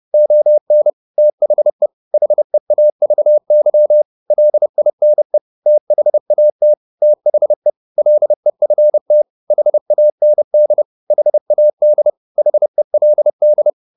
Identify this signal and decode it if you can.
Morse code, heard as a keyed tone, about 30 wpm, 600 Hz. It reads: ON THE HEAVY LINE THAT THE LEFT HAND HAD HELD
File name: the old man and the sea / the old man and the sea 1 30wpm